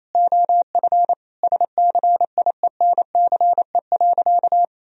Morse code, heard as a keyed tone, about 28 wpm, 700 Hz. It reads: OF SCIENCE.
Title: Sentences from Top 1000 Words - Review Part 4 - 28wpm